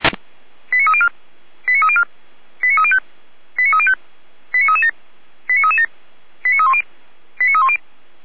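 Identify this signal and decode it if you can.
Five-tone selective call (ZVEI), heard as quick runs of five tones, tones of about 70 ms. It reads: 89304 89304 89305 89305 89337 89337 89310 89310
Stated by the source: Received on an SDR.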